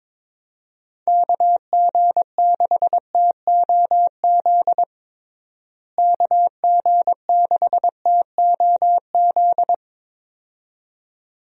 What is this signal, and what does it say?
Morse code, heard as a keyed tone, about 22 wpm, 700 Hz. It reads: KG6TOZ KG6TOZ
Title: USA Call Signs - 6 Characters - Repeated 2x - 22wpm